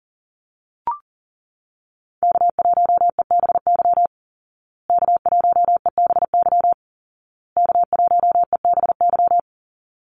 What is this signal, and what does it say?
Morse code, heard as a keyed tone, about 40 wpm, 700 Hz. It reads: K1EBY K1EBY K1EBY
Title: Call Signs Repeated 2x - 40wpm